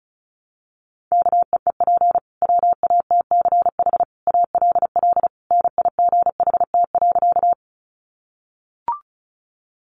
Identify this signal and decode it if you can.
Morse code, heard as a keyed tone, about 34 wpm, 700 Hz. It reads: KEEP WATCH ALL NIGHT.
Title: Sentences from Top 300 Words 35wpm